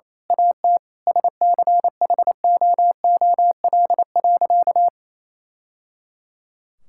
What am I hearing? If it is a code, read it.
Morse code, heard as a keyed tone, about 28 wpm, 700 Hz. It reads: AT SCHOOL.
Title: Sentences from Top 1000 Words - Review Part 6 - 28wpm